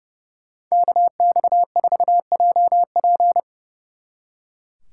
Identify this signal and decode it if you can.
Morse code, heard as a keyed tone, about 37 wpm, 690 Hz. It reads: KX4JP
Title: Speed Racing - USA Call Signs 30wpm